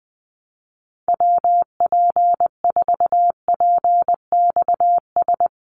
Morse code, heard as a keyed tone, about 20 wpm, 700 Hz. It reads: WP4PXS
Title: USA Call Signs 20wpm